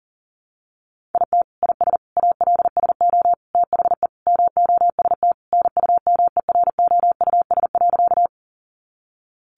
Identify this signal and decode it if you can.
Morse code, heard as a keyed tone, about 40 wpm, 700 Hz. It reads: IT IS ALSO THE MOST NUMEROUS.
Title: Sentences from Top 1400 Words 40wpm